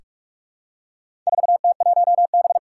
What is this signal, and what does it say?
Morse code, heard as a keyed tone, about 46 wpm, 700 Hz. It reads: 4T1B
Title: Four Challenging Letter-Number Combinations 45wpm